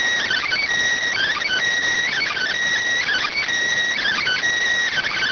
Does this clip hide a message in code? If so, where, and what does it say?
Yes: Emergency Alert System (SAME) tones from 0.2 to 4.4 s, heard as harsh buzzing bursts.